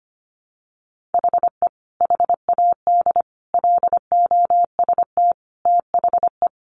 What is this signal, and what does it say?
Morse code, heard as a keyed tone, about 25 wpm, 700 Hz. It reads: HE HAD LOST THE
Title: Sentences from Top 1500 Words 25wpm